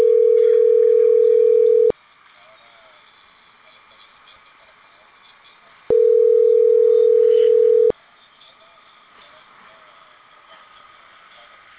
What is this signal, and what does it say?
Ringback tone (North America), 440 + 480 Hz.